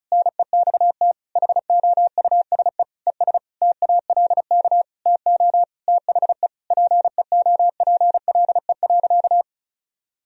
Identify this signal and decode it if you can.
Morse code, heard as a keyed tone, about 34 wpm, 700 Hz. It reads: NEXT HOUSE ES TALK TO THE PEOPLE.
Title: Sentences from Top 400 Words and Abbreviations 35wpm